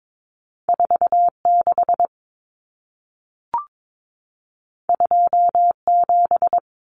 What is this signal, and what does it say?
Morse code, heard as a keyed tone, about 22 wpm, 700 Hz. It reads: 46 27
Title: Numbers - 2-Digit 22wpm